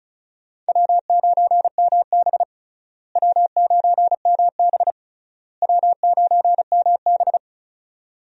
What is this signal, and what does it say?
Morse code, heard as a keyed tone, about 36 wpm, 700 Hz. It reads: W9MB W9MB W9MB
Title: USA Call Signs - 4 Characters - Repeated 3x - 35wpm